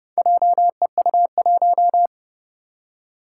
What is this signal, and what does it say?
Morse code, heard as a keyed tone, about 31 wpm, 700 Hz. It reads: JEU1
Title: Four Challenging Letter-Number Combinations 30wpm